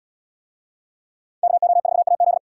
Morse code, heard as a keyed tone, about 72 wpm, 700 Hz. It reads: KO4AZ